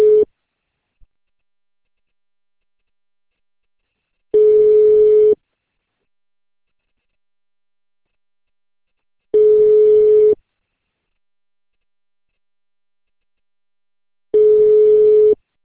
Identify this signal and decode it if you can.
Ringback tone (Europe, 425 Hz).